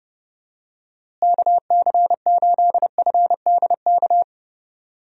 Morse code, heard as a keyed tone, about 30 wpm, 700 Hz. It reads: KC8FDK
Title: USA Call Signs - 6 Characters - Repeated 3x - 30wpm